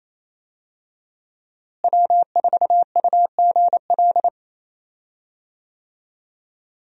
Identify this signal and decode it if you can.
Morse code, heard as a keyed tone, about 28 wpm, 700 Hz. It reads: W4UGL
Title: USA Call Signs - 5 Characters - Repeated 3x - 28wpm